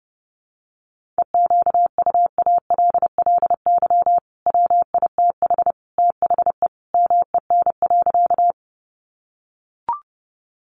Morse code, heard as a keyed tone, about 30 wpm, 700 Hz. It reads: EQUALLY WITH THE MEN.
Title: Sentences from Top 1400 Words 30wpm